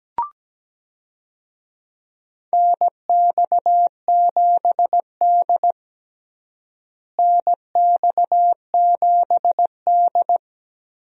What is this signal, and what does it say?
Morse code, heard as a keyed tone, about 17 wpm, 700 Hz. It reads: NX7D NX7D
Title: USA Call Signs - 4 Characters - Repeated 3x - 17wpm